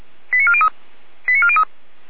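Five-tone selective call (ZVEI), heard as quick runs of five tones, tones of about 70 ms. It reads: 89403 89403
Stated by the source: Five-tone ZVEI call 89402 received on an SDR.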